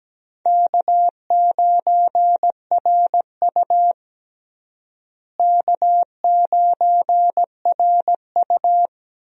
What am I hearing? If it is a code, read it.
Morse code, heard as a keyed tone, about 17 wpm, 700 Hz. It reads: K9RU K9RU
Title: USA Call Signs - 4 Characters - Repeated 2x - 17wpm